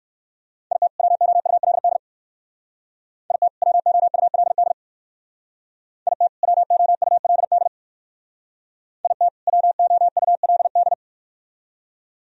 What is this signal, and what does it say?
Morse code, heard as a keyed tone, about 57 wpm, 690 Hz. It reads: IT WOULD IT WOULD IT WOULD IT WOULD